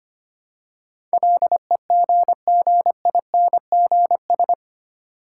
Morse code, heard as a keyed tone, about 25 wpm, 700 Hz. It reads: LEGGINGS
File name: practice / 25wpm / Difficult Words with Hints - 25wpm